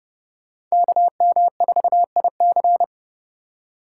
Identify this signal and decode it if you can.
Morse code, heard as a keyed tone, about 39 wpm, 700 Hz. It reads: KM4IC